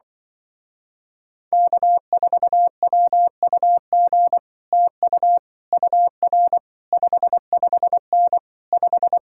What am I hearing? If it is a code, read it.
Morse code, heard as a keyed tone, about 24 wpm, 700 Hz. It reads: K4WUG TU UR 55N 5
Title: Speed-Racing Common QSO Phrases - 20wpm